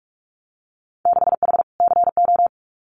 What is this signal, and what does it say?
Morse code, heard as a keyed tone, about 46 wpm, 700 Hz. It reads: 6H CO
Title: Winter Field Day Exchanges Encore 45wpm